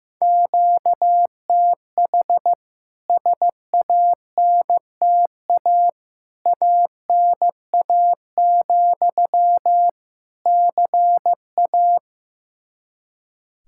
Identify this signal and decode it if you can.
Morse code, heard as a keyed tone, about 16 wpm, 700 Hz. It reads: QTH SANTA ANA, CA
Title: K1USN SST QSO Practice 15wpm